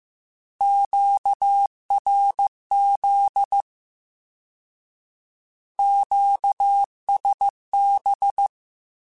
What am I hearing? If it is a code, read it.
Morse code, heard as a keyed tone, about 15 wpm, 800 Hz. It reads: QRZ QSB